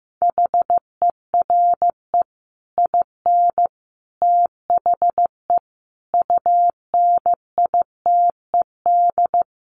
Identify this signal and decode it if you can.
Morse code, heard as a keyed tone, about 15 wpm, 700 Hz. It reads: HERE IN THE UNITED
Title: Sets of 5 Words 15wpm